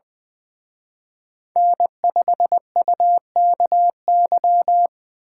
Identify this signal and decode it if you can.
Morse code, heard as a keyed tone, about 24 wpm, 700 Hz. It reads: N5UKY
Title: Speed Racing - USA 1x3 Call Signs 20wpm